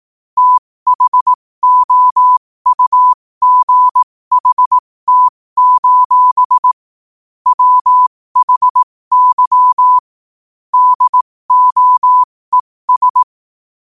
Morse code, heard as a keyed tone, about 18 wpm, spaced as for 15 wpm, 1000 Hz. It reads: THOUGHT: WHY DOES
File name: qotdfiles / 15wpm